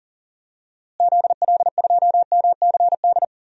Morse code, heard as a keyed tone, about 37 wpm, 700 Hz. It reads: ZL2MCD
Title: DXCC-100 Call Signs - 6 Characters - Repeated 2x - 40wpm